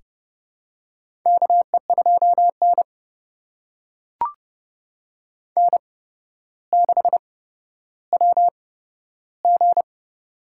Morse code, heard as a keyed tone, about 30 wpm, spaced as for 13 wpm, 700 Hz. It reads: KE2N N6WG
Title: USA Call Signs - 4 Characters - 8x Character Spacing - 30wpm